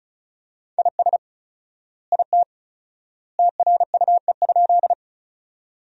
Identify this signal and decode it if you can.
Morse code, heard as a keyed tone, about 36 wpm, 700 Hz. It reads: IS IT TRUE?
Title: 4x Word-Spacing - Sentences from Top 400 Words - 35wpm